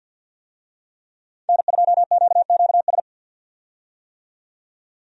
Morse code, heard as a keyed tone, about 50 wpm, 700 Hz. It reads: N2QQS
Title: USA Call Signs - 5 Characters - 50wpm